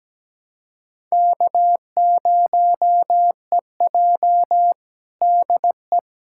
Morse code, heard as a keyed tone, about 17 wpm, 700 Hz. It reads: K0EJ DE
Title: Practice QSO - Exchange 4 - Ending - 17wpm